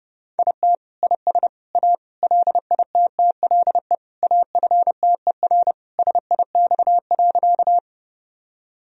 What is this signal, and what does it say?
Morse code, heard as a keyed tone, about 30 wpm, 700 Hz. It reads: IT IS A LITTLE AFTER SIX.
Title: Sentences from Top 1000 Words - Review Part 2 - 30wpm